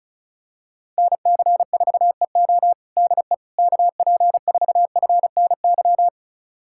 Morse code, heard as a keyed tone, about 36 wpm, 700 Hz. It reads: NC4EO DE KP4FNY
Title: USA Callsign DE Callsign 35wpm